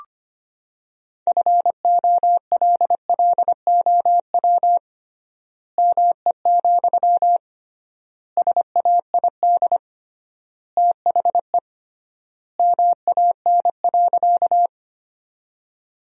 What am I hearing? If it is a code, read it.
Morse code, heard as a keyed tone, about 25 wpm, 700 Hz. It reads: FOLLOW ME, SAID THE MAN.
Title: 3x Word-Spacing - Sentences from Top 200 Words - 25wpm